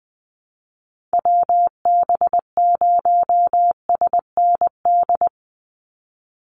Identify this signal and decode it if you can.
Morse code, heard as a keyed tone, about 20 wpm, 700 Hz. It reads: WB0SND
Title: Call Signs Repeated 3x - 20wpm